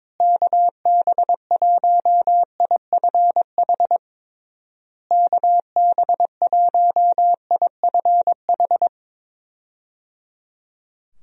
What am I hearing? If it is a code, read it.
Morse code, heard as a keyed tone, about 21 wpm, 700 Hz. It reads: KB1IFH KB1IFH
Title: USA Call Signs - 6 Characters - Repeated 3x - 22wpm